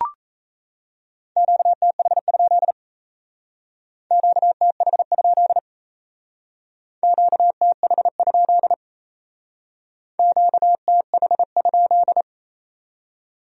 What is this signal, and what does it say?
Morse code, heard as a keyed tone, about 35 wpm, 700 Hz. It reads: QTH? QTH? QTH? QTH?
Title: Speed-Racing Q-Codes - 28wpm